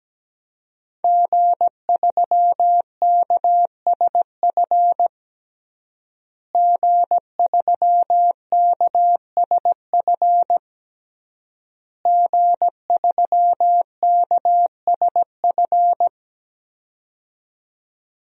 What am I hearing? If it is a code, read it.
Morse code, heard as a keyed tone, about 17 wpm, 700 Hz. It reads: G3KSF G3KSF G3KSF
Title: DXCC-100 Call Signs - 5 Characters - Repeated 3x - 17wpm